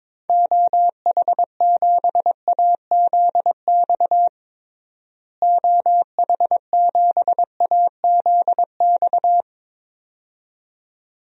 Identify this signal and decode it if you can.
Morse code, heard as a keyed tone, about 22 wpm, 700 Hz. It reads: OH7AZX OH7AZX
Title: DXCC-100 Call Signs - 6 Characters - Repeated 2x - 22wpm